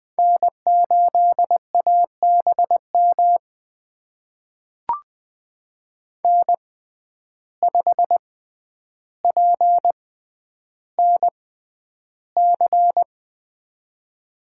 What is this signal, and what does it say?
Morse code, heard as a keyed tone, about 20 wpm, spaced as for 10 wpm, 700 Hz. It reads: N8ABM N5PNC
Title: USA Call Signs - 5 Characters - 6x Character Spacing - 20wpm